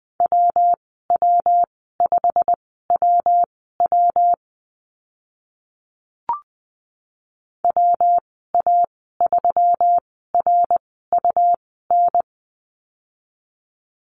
Morse code, heard as a keyed tone, about 20 wpm, spaced as for 14 wpm, 700 Hz. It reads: WW5WW WA3RUN
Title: 122 Lesson 20g - Number 3 - Call Signs 20wpm